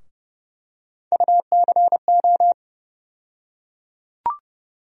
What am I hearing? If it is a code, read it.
Morse code, heard as a keyed tone, about 30 wpm, 700 Hz. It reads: UCO
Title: USA Call Sign Postfixes - 3 Letters 30wpm